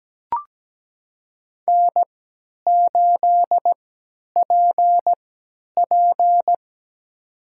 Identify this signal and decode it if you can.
Morse code, heard as a keyed tone, about 17 wpm, spaced as for 10 wpm, 700 Hz. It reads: N8PP
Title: USA Call Signs - 3x Character Spacing - 17wpm